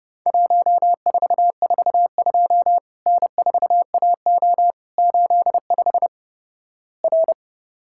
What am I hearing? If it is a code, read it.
Morse code, heard as a keyed tone, about 28 wpm, 700 Hz. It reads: 1442 N4AO 85 R
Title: Worked All Europe DX Contest - QTC Exchange - 30wpm